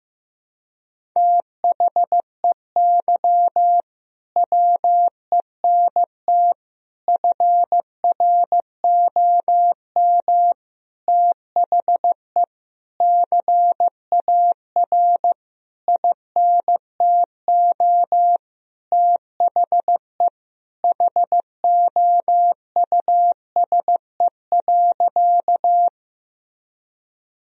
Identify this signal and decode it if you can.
Morse code, heard as a keyed tone, about 15 wpm, 700 Hz. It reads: THEY WENT FROM THE CAR INTO THE HOUSE.
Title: Sentences from Top 1000 Words - Review Part 1 - 15wpm